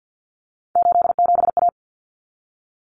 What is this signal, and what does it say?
Morse code, heard as a keyed tone, about 48 wpm, 700 Hz. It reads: 87A
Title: Common Cut Numbers - 3-Digit 50wpm